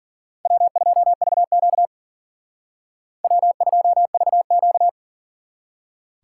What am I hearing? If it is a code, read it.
Morse code, heard as a keyed tone, about 52 wpm, 700 Hz. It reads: W2VQ W2VQ